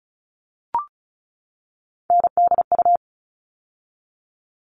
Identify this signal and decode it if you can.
Morse code, heard as a keyed tone, about 34 wpm, 700 Hz. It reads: NDU
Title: All Cut Numbers - 3-Digit 35wpm